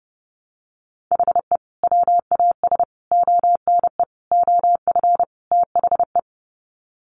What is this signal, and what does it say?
Morse code, heard as a keyed tone, about 30 wpm, 700 Hz. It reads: HE WAS ONE OF THE
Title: Sets of 5 Words 30wpm